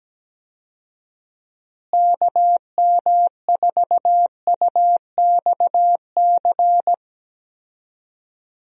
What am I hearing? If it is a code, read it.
Morse code, heard as a keyed tone, about 17 wpm, 700 Hz. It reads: KM4UXC
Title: USA Call Signs - 6 Characters - Repeated 2x - 17wpm